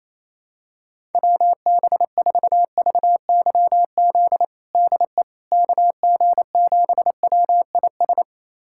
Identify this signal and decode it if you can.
Morse code, heard as a keyed tone, about 27 wpm, 700 Hz. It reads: WB4VYZ DE KG7WIS
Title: USA Callsign DE Callsign 28wpm